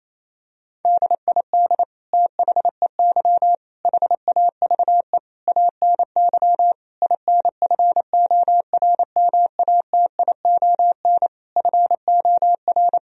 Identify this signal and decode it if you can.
Morse code, heard as a keyed tone, about 28 wpm, 700 Hz. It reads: DID THEY HAVE ANY INFORMATION FOR
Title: Sentences from Top 1000 Words - Review Part 3 - 28wpm